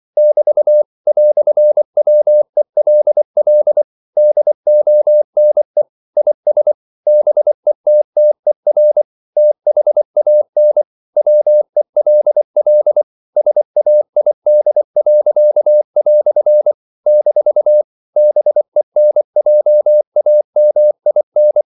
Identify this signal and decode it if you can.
Morse code, heard as a keyed tone, about 24 wpm, 600 Hz. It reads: <BT> "WELL DONE IS BETTER THAN WELL SAID." - BENJAMIN
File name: CW / Practice / EnglishQuotations0424